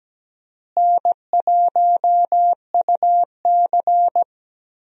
Morse code, heard as a keyed tone, about 17 wpm, 700 Hz. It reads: N1UC
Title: USA Call Signs - 4 Characters - Repeated 2x - 17wpm